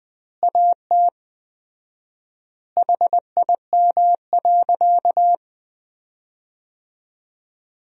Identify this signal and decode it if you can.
Morse code, heard as a keyed tone, about 20 wpm, 700 Hz. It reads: AT HIM.
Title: 4x Word-Spacing - Sentences from Top 100 Words - 20wpm